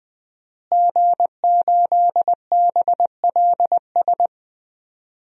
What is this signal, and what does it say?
Morse code, heard as a keyed tone, about 20 wpm, 700 Hz. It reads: G8BLS
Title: DXCC-100 Call Signs - 5 Characters - Repeated 2x - 20wpm